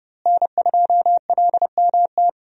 Morse code, heard as a keyed tone, about 30 wpm, 700 Hz. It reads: N2LMT
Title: USA Call Signs - 5 Characters - Repeated 3x - 30wpm